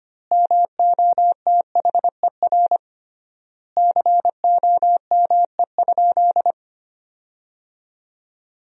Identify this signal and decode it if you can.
Morse code, heard as a keyed tone, about 25 wpm, 700 Hz. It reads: MOTHER COME?
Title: 3x Word-Spacing - Sentences from Top 200 Words - 25wpm